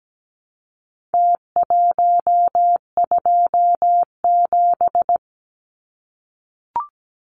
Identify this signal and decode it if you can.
Morse code, heard as a keyed tone, about 17 wpm, 700 Hz. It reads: T127
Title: Numbers - Contest Style - 4-Digit 17wpm